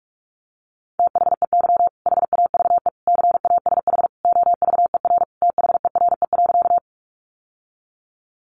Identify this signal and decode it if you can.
Morse code, heard as a keyed tone, about 44 wpm, 700 Hz. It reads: THEY HAVE CASH OVER THERE.
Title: Sentences from Top 700 Words 45wpm